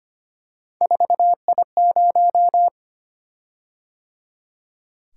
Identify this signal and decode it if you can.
Morse code, heard as a keyed tone, about 25 wpm, 700 Hz. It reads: 4I0
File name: Three Letter-Number Combinations - 25wpm